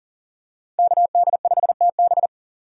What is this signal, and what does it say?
Morse code, heard as a keyed tone, about 40 wpm, 700 Hz. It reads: KD5TB
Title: USA Call Signs - 5 Characters - 2x Character Spacing - 40wpm